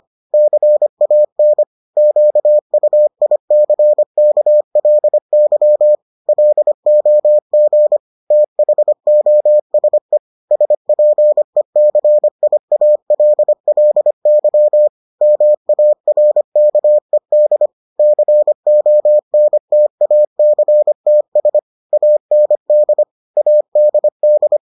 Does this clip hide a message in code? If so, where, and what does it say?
Yes: Morse code from 0.3 to 24.6 s, heard as a keyed tone, about 25 wpm, 600 Hz. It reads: CAN QUICKLY LOG THOSE SPECIALLY MARKED CONTACTS AND ADD